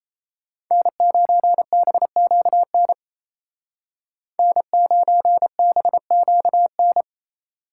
Morse code, heard as a keyed tone, about 36 wpm, 700 Hz. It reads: N9BQN N9BQN